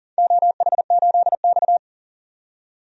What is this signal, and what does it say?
Morse code, heard as a keyed tone, about 42 wpm, 700 Hz. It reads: OH8X
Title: DXCC-100 Call Signs - 4 Characters - 40wpm